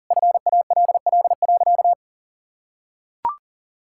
Morse code, heard as a keyed tone, about 40 wpm, 700 Hz. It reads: FALL.
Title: Sentences from Top 1000 Words - Review Part 6 - 40wpm